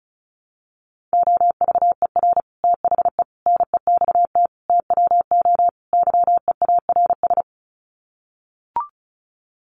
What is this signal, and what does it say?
Morse code, heard as a keyed tone, about 34 wpm, 700 Hz. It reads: OVER THE NEXT TWO YEARS
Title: Sets of 5 Words Encore2 35wpm